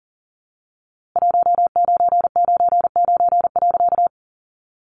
Morse code, heard as a keyed tone, about 40 wpm, 700 Hz. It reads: 1999.
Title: A Sound Of Thunder 40wpm